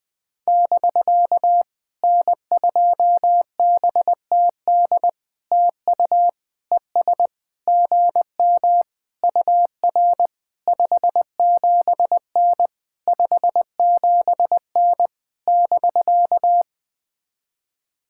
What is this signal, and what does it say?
Morse code, heard as a keyed tone, about 20 wpm, 700 Hz. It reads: <BK> N2BTD TU ES GM UR 57N 57N <BK>
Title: POTA QSO Practice - 20wpm